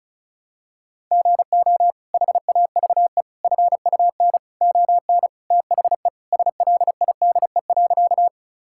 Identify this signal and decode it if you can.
Morse code, heard as a keyed tone, about 34 wpm, 700 Hz. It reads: GO HAVE FUN ON THE SLIDE.
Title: Sentences from Top 1600 Words 35wpm